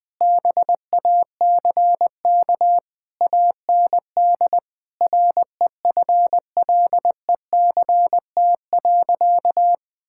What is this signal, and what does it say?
Morse code, heard as a keyed tone, about 20 wpm, 700 Hz. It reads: BACK AND REFLECT.
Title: Sentences with 6-Word Phrases - 20wpm